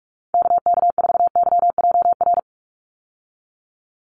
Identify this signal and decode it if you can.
Morse code, heard as a keyed tone, about 44 wpm, 700 Hz. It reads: KK4YJR
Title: USA Call Signs 45wpm